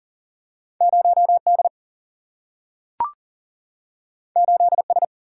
Morse code, heard as a keyed tone, about 39 wpm, 700 Hz. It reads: 0 D 8 S
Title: Single-Letter Single-Number 40wpm